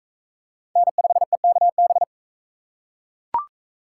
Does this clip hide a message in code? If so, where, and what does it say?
Yes: Morse code from 0.8 to 2.0 s, heard as a keyed tone, about 41 wpm, 700 Hz. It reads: N5EKB